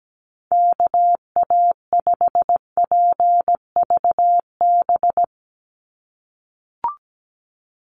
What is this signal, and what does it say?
Morse code, heard as a keyed tone, about 17 wpm, 700 Hz. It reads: KA5PVB
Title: NA SOTA Chaser Callsigns 17wpm